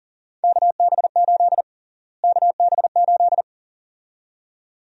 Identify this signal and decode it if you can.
Morse code, heard as a keyed tone, about 40 wpm, 700 Hz. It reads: KB8 KB8
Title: USA Call Sign Prefixes - Repeated 2x - 40wpm